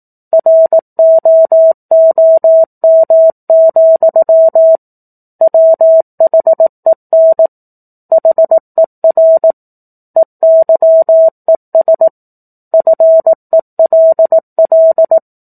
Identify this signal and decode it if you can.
Morse code, heard as a keyed tone, about 18 wpm, 650 Hz. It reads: ROOM, WHEN HER EYES FELL
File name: 18 WPM